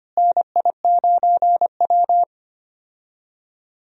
Morse code, heard as a keyed tone, about 25 wpm, 700 Hz. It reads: NI9W
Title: USA Call Signs - 4 Characters - Repeated 3x - 25wpm